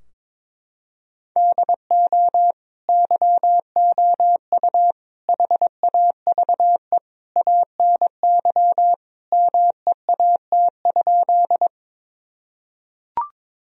Morse code, heard as a keyed tone, about 21 wpm, 700 Hz. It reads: DO YOU HAVE ANY MEAT?
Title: Sentences from Top 1000 Words - Review Part 3 - 22wpm